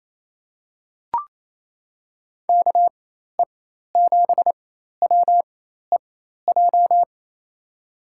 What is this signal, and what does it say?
Morse code, heard as a keyed tone, about 28 wpm, spaced as for 13 wpm, 700 Hz. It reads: KE7WEJ
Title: USA Call Signs - 6 Characters - 4x Character Spacing - 28wpm